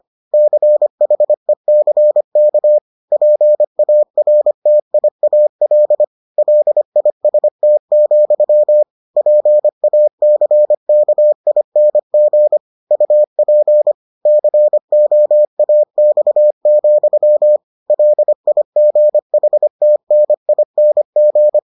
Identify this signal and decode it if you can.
Morse code, heard as a keyed tone, about 25 wpm, 600 Hz. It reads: CHECK PARTIAL LIST, PACKING UP COAX, LIGHTNING